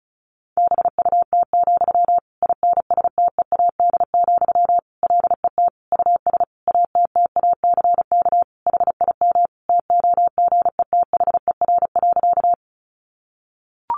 Morse code, heard as a keyed tone, about 34 wpm, 700 Hz. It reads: BUT, INSTEAD, LET US ATTACK HIM TOGETHER.
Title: Sentences from Top 1500 Words 35wpm